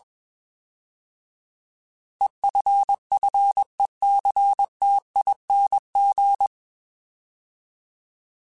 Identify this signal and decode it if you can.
Morse code, heard as a keyed tone, about 21 wpm, 800 Hz. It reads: EFFECTING
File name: Alphabetical / Alphabetical mp3s / Es @20